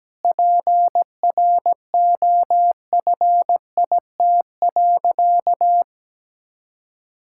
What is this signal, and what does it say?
Morse code, heard as a keyed tone, about 17 wpm, 700 Hz. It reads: PROFIT.
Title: Sentences from Top 1000 Words - Review Part 5 - 17wpm